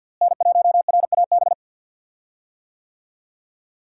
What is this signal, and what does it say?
Morse code, heard as a keyed tone, about 51 wpm, 700 Hz. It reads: N1RAB
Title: USA Call Signs - 5 Characters - Repeated 2x - 50wpm